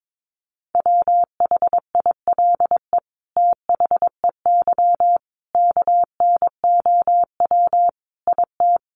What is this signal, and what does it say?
Morse code, heard as a keyed tone, about 22 wpm, 700 Hz. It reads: WHILE THEY KNOW IT
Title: Sentences from Top 1400 Words 22wpm